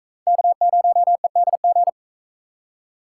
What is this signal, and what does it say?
Morse code, heard as a keyed tone, about 43 wpm, 700 Hz. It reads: K0EDG